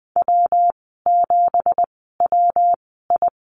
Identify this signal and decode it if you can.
Morse code, heard as a keyed tone, about 20 wpm, spaced as for 15 wpm, 700 Hz. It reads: W7WI
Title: 193 Lesson 31g - Letter v - Call Signs 20wpm